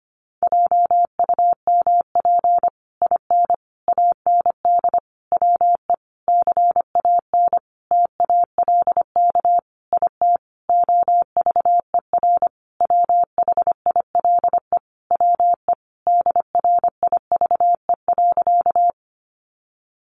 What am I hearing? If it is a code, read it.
Morse code, heard as a keyed tone, about 25 wpm, 700 Hz. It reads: JUMP IN AND WE CAN TALK IT OVER WHILE WE DRIVE.
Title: Sentences from Top 1300 Words 25wpm